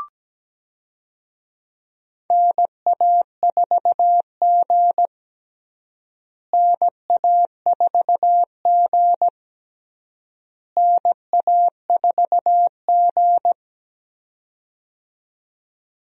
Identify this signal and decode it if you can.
Morse code, heard as a keyed tone, about 17 wpm, 700 Hz. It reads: NA4G NA4G NA4G
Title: USA Call Signs - 4 Characters - Repeated 3x - 17wpm